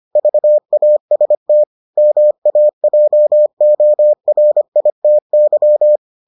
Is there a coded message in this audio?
Yes: Morse code from 0.2 to 6.0 s, heard as a keyed tone, about 25 wpm, 600 Hz. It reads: VAST MAJORITY